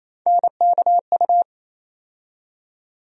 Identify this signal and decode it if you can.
Morse code, heard as a keyed tone, about 28 wpm, 700 Hz. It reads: NKU
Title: USA Call Sign Postfixes - 3 Letters 28wpm